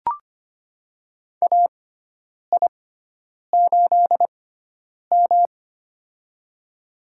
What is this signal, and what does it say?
Morse code, heard as a keyed tone, about 25 wpm, spaced as for 9 wpm, 700 Hz. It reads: AI8M
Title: USA Call Signs - 4 Characters - 6x Character Spacing - 25wpm